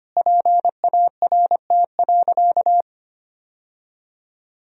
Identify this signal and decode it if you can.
Morse code, heard as a keyed tone, about 25 wpm, 700 Hz. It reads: PART.
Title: Sentences from Top 1000 Words - Review Part 1 - 25wpm